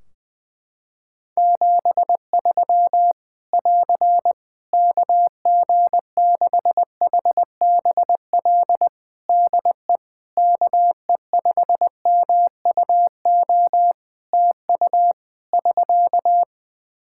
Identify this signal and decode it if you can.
Morse code, heard as a keyed tone, about 20 wpm, 700 Hz. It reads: 73 <AR> KG6HBL DE KE5MUO TU <SK>
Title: Practice QSO - Exchange 4 - Ending - 20wpm